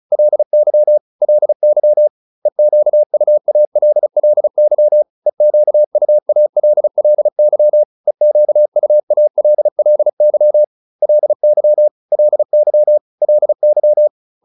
Morse code, heard as a keyed tone, about 34 wpm, 600 Hz. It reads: LY LY EQUALLY EQUALLY EQUALLY LY LY LY
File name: LY7 35